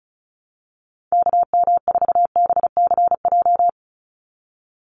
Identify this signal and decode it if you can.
Morse code, heard as a keyed tone, about 36 wpm, 700 Hz. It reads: KM4BCJ
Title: USA Call Signs 35wpm